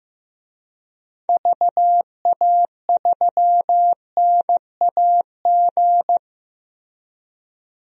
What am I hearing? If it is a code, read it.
Morse code, heard as a keyed tone, about 15 wpm, 700 Hz. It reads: VA3NAG
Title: DXCC-100 Call Signs - 6 Characters - Repeated 3x - 15wpm